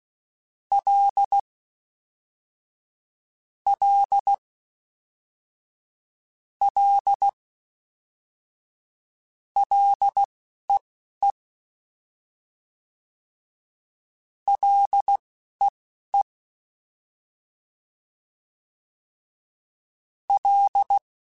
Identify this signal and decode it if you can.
Morse code, heard as a keyed tone, about 16 wpm, spaced as for 11 wpm, 800 Hz. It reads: L L L LEE LEE L